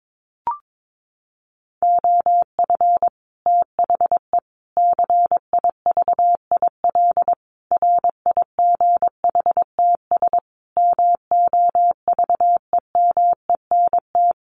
Morse code, heard as a keyed tone, about 22 wpm, 700 Hz. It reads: OF THE CIVIL RIGHTS MOVEMENT
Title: Sets of 5 Words Encore2 22wpm